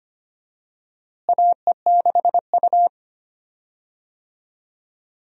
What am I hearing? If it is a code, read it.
Morse code, heard as a keyed tone, about 25 wpm, 700 Hz. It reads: AE6U
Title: USA Call Signs - 4 Characters - Repeated 2x - 25wpm